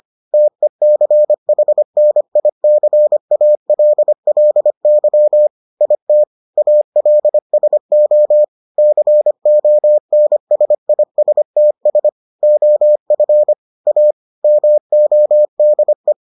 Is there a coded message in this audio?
Yes: Morse code from 0.3 to 16.1 s, heard as a keyed tone, about 25 wpm, 600 Hz. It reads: TECHNICALLY IT ALSO CONSISTS OF A MODE